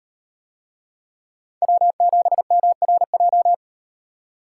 Morse code, heard as a keyed tone, about 36 wpm, 700 Hz. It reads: W7MRJ